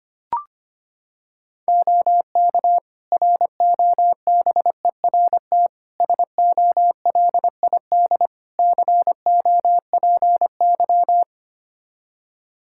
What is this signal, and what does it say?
Morse code, heard as a keyed tone, about 25 wpm, 700 Hz. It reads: OK ROBERT SOLID COPY
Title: K1USN SST QSO Practice 25wpm